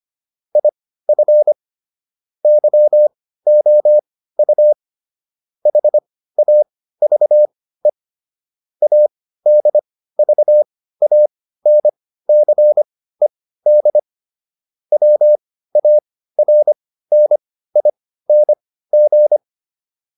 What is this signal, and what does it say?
Morse code, heard as a keyed tone, about 25 wpm, spaced as for 15 wpm, 600 Hz. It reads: IF YOU HAVE ADVANCED WARNING